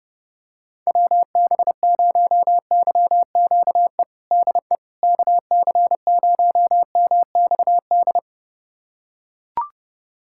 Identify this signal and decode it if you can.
Morse code, heard as a keyed tone, about 32 wpm, 700 Hz. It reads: WB0YQE DE KC0MXD
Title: USA Callsign DE Callsign - 6 Characters - 30wpm